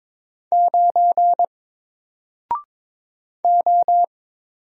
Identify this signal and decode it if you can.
Morse code, heard as a keyed tone, about 22 wpm, 700 Hz. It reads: 9 O
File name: Single-Letter-Number-22wpm